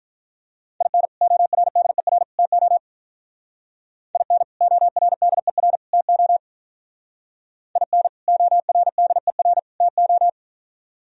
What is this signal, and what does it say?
Morse code, heard as a keyed tone, about 44 wpm, 700 Hz. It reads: IN ORDER TO IN ORDER TO IN ORDER TO